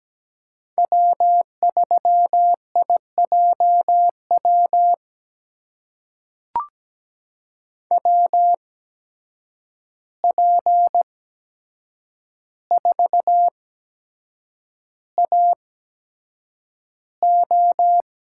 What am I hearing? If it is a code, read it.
Morse code, heard as a keyed tone, about 17 wpm, spaced as for 7 wpm, 700 Hz. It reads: W3IJW WP4AO
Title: USA Call Signs - 8x Character Spacing - 17wpm